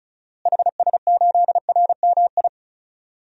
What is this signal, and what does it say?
Morse code, heard as a keyed tone, about 36 wpm, 700 Hz. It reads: HS8RMI
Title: DXCC-100 Call Signs - 6 Characters - Repeated 3x - 35wpm